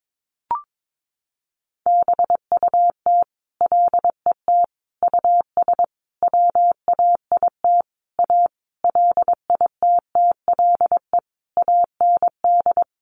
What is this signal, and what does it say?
Morse code, heard as a keyed tone, about 22 wpm, 700 Hz. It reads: BUT LET US WAIT A LITTLE AND
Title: Sentences from Top 1500 Words 22wpm